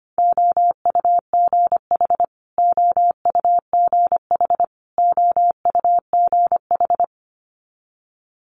Morse code, heard as a keyed tone, about 25 wpm, 700 Hz. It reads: OUGH OUGH OUGH
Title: Multi-letter Phonemes Repeated 3x 25wpm